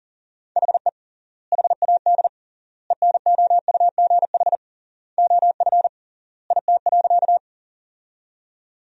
Morse code, heard as a keyed tone, about 40 wpm, 700 Hz. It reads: HE HAD ENOUGH OF IT.
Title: 3x Word-Spacing - Sentences from Top 400 Words - 40wpm